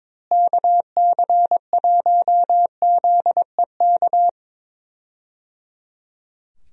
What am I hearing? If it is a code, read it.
Morse code, heard as a keyed tone, about 22 wpm, 700 Hz. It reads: KC1ZEK
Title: USA Call Signs - 6 Characters - Repeated 3x - 22wpm